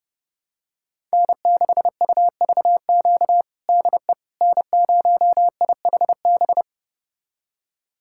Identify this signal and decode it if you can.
Morse code, heard as a keyed tone, about 30 wpm, 700 Hz. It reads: N6UVQ DE N0IHB
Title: USA Callsign DE Callsign - 5 Characters - 30wpm